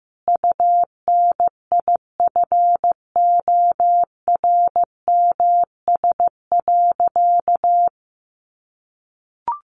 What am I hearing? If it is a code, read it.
Morse code, heard as a keyed tone, about 15 wpm, 700 Hz. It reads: UNIFORMS.
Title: Easy Phrases 15wpm